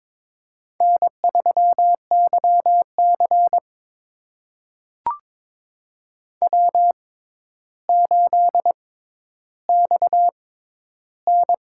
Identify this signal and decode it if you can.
Morse code, heard as a keyed tone, about 22 wpm, spaced as for 11 wpm, 700 Hz. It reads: N3YC W8XN
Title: USA Call Signs - 4 Characters - 6x Character Spacing - 22wpm